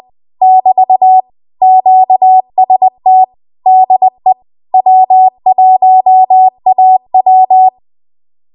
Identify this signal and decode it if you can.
Morse code, heard as a keyed tone, about 20 wpm, 750 Hz. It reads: <BT> QST DE W1AW
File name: Archive / 20 WPM / 20WPM